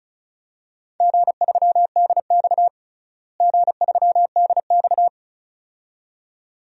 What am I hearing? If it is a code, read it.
Morse code, heard as a keyed tone, about 36 wpm, 700 Hz. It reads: G3DX G3DX
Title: DXCC-100 Call Signs - 4 Characters - Repeated 2x - 35wpm